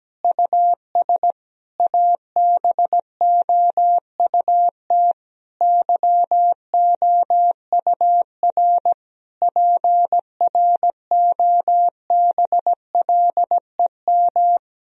Morse code, heard as a keyed tone, about 17 wpm, 700 Hz. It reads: US ABOUT YOUR PROBLEM
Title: Sentences from Top 1000 Words - Review Part 3 - 17wpm